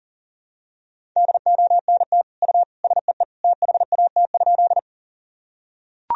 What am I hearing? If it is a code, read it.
Morse code, heard as a keyed tone, about 40 wpm, 700 Hz. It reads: DONT U SEE THAT?
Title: Sentences from Top 300 Words and Abbreviations 40wpm